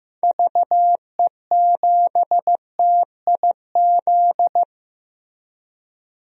Morse code, heard as a keyed tone, about 15 wpm, 700 Hz. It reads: VE7TIZ
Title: DXCC-100 Call Signs - 6 Characters - Repeated 2x - 15wpm